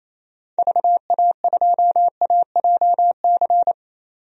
Morse code, heard as a keyed tone, about 28 wpm, 700 Hz. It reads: VA2AJC
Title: DXCC-100 Call Signs - 6 Characters - 28wpm